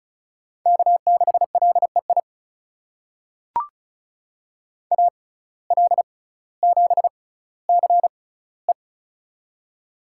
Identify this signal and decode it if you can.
Morse code, heard as a keyed tone, about 36 wpm, spaced as for 18 wpm, 700 Hz. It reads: K6LEI AL7CE
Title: USA Call Signs - 5 Characters - 6x Character Spacing - 35wpm